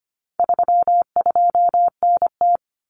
Morse code, heard as a keyed tone, about 25 wpm, 700 Hz. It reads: 32NT
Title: Common Cut Numbers - 4-Digit 25wpm